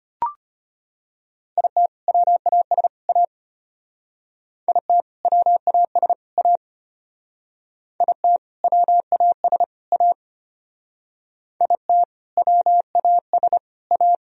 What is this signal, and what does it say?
Morse code, heard as a keyed tone, about 32 wpm, 700 Hz. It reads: IT WAS A IT WAS A IT WAS A IT WAS A